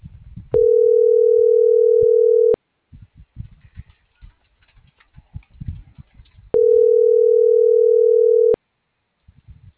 Ringback tone (North America), 440 + 480 Hz.